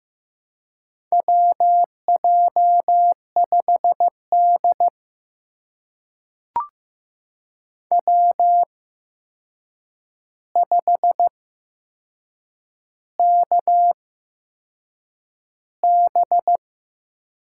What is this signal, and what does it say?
Morse code, heard as a keyed tone, about 15 wpm, spaced as for 6 wpm, 700 Hz. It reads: WJ5D W5KB
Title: USA Call Signs - 8x Character Spacing - 15wpm